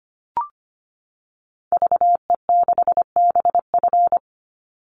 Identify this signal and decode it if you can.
Morse code, heard as a keyed tone, about 25 wpm, 700 Hz. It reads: VE6BF
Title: Call Signs Repeated 2x - 25wpm